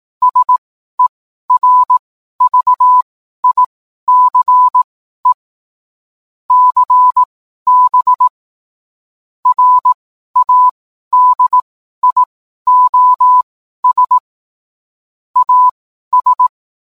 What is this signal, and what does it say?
Morse code, heard as a keyed tone, about 18 wpm, spaced as for 12 wpm, 1000 Hz. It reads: SERVICE CB RADIOS AS